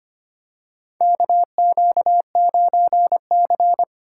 Morse code, heard as a keyed tone, about 25 wpm, 700 Hz. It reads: KQ9C
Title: USA Call Signs - 4 Characters - 3x Character Spacing - 25wpm